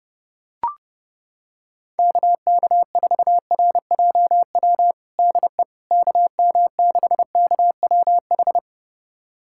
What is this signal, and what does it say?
Morse code, heard as a keyed tone, about 30 wpm, 700 Hz. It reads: KK4RJW DE KM6KWH
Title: USA Callsign DE Callsign 30wpm